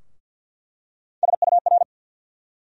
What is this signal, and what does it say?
Morse code, heard as a keyed tone, about 47 wpm, 690 Hz. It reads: SUR
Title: USA Call Sign Postfixes - 3 Letters 50wpm